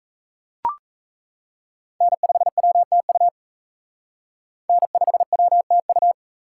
Morse code, heard as a keyed tone, about 34 wpm, 700 Hz. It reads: N5WTU N5WTU
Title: Speed Racing - USA 1x3 Call Signs 28wpm